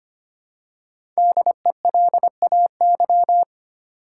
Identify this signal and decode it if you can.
Morse code, heard as a keyed tone, about 25 wpm, 700 Hz. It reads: DELAY
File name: 5 Letter Words - First 3 Letters Given - 25wpm